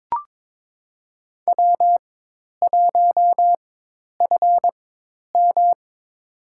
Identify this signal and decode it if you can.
Morse code, heard as a keyed tone, about 22 wpm, spaced as for 10 wpm, 700 Hz. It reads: W1FM
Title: USA Call Signs - 4 Characters - 4x Character Spacing - 22wpm